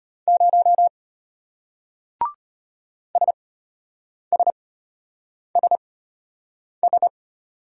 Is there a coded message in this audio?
Yes: Morse code from 0.3 to 7.1 s, heard as a keyed tone, about 34 wpm, spaced as for 8 wpm, 700 Hz. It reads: 0 SSSS